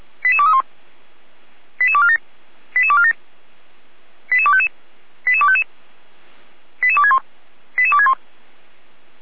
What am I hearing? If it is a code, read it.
Five-tone selective call (ZVEI), heard as quick runs of five tones, tones of about 70 ms. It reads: 88231 88247 88247 88255 88255 88261 88261